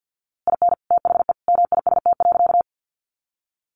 Morse code, heard as a keyed tone, about 50 wpm, 700 Hz. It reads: IN THE MIST.
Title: A Sound Of Thunder 50wpm